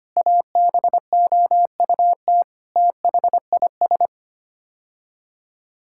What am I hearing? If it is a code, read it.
Morse code, heard as a keyed tone, about 25 wpm, 700 Hz. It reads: ABOUT THIS
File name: Sets of 9 Words - 25wpm